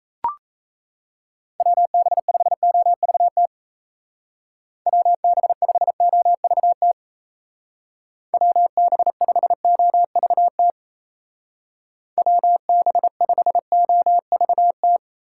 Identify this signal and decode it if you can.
Morse code, heard as a keyed tone, about 35 wpm, 700 Hz. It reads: WB5OVT WB5OVT WB5OVT WB5OVT
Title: Speed Racing - USA Call Signs 28wpm